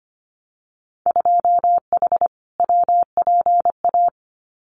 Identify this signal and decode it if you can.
Morse code, heard as a keyed tone, about 25 wpm, 700 Hz. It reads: 2H WPA
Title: Winter Field Day Exchanges 25wpm